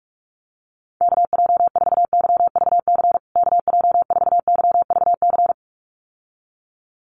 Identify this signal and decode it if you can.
Morse code, heard as a keyed tone, about 44 wpm, 690 Hz. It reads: KJ4YVC KJ4YVC
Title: USA Call Signs Repeated 2x 45wpm